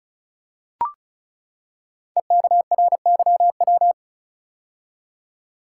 Morse code, heard as a keyed tone, about 34 wpm, 700 Hz. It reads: EKRYW
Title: Code Groups - 5 Letters - 35wpm